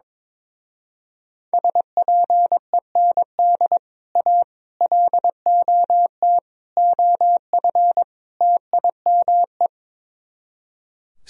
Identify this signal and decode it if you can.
Morse code, heard as a keyed tone, about 23 wpm, 700 Hz. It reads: SPEND A LOT OF TIME
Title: Speed-Racing - Sets of 5 Words 22wpm